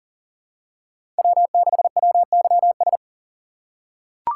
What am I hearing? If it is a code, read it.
Morse code, heard as a keyed tone, about 40 wpm, 700 Hz. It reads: W6WYS
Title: USA Call Signs - 5 Characters - 3x Character Spacing - 40wpm